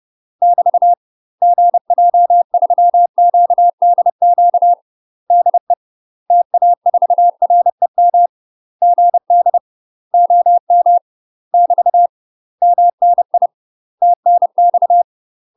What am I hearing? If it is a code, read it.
Morse code, heard as a keyed tone, about 30 wpm, 700 Hz. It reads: <BT> GJ3QDQ DE TA4REM GD OM <BT> MNI TNX
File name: UK MASTER QSOs TRAINING / 30WPM / UK MASTER QSOs 0008